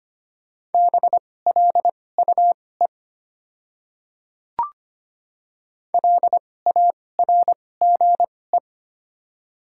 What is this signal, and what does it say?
Morse code, heard as a keyed tone, about 25 wpm, spaced as for 18 wpm, 700 Hz. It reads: BLUE LARGE
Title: 2x Character Spacing - Top 500 Words - 25wpm